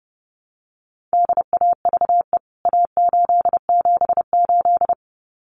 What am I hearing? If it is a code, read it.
Morse code, heard as a keyed tone, about 30 wpm, 700 Hz. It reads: DAVE A878
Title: CWT Exchange 30wpm